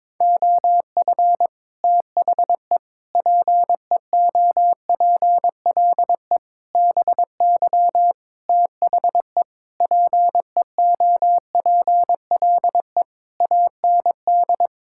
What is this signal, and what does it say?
Morse code, heard as a keyed tone, about 22 wpm, 700 Hz. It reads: OF THE PEOPLE BY THE PEOPLE AND
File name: Sets of 9 Words - 22wpm